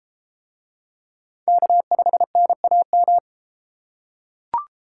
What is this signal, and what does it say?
Morse code, heard as a keyed tone, about 29 wpm, 700 Hz. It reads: K5NAM